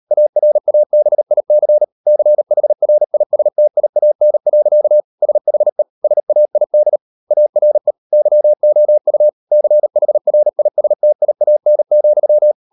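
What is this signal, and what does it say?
Morse code, heard as a keyed tone, about 38 wpm, 600 Hz. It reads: ARABIC CHRISTIAN. SHE SAID ARE YOU CHRISTIAN,
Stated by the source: Morse code practice at 38 WPM, sending Ken Follett's novel Never in English, from CW med Gnister.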